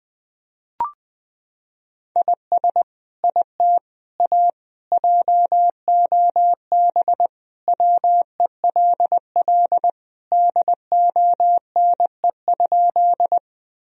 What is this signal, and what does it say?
Morse code, heard as a keyed tone, about 20 wpm, 700 Hz. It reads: IS IT A JOB WELL DONE?
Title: Sentences from Top 1000 Words - Review Part 3 - 20wpm